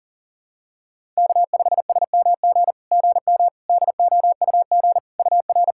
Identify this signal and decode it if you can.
Morse code, heard as a keyed tone, about 44 wpm, 700 Hz. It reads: K5SMG GM DOUG UR
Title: SOTA QSO Practice 40wpm